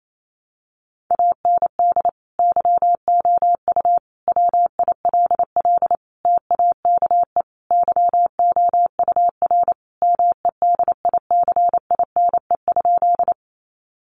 Morse code, heard as a keyed tone, about 28 wpm, 700 Hz. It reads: AND YOU WILL TAKE YOUR MEDICINE?
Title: Sentences from Top 900 Words 28wpm